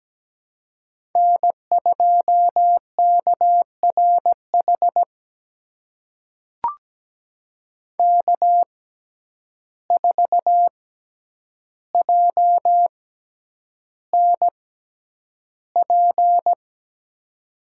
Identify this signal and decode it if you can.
Morse code, heard as a keyed tone, about 17 wpm, spaced as for 9 wpm, 700 Hz. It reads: N2KRH K4JNP
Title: USA Call Signs - 5 Characters - 6x Character Spacing - 17wpm